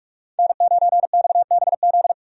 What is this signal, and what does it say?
Morse code, heard as a keyed tone, about 42 wpm, 700 Hz. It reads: N9XBZ
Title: USA Call Signs - 5 Characters - 45wpm